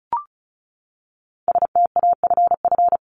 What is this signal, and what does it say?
Morse code, heard as a keyed tone, about 34 wpm, 700 Hz. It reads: STAFF
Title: Top 1001-1250 Words 35wpm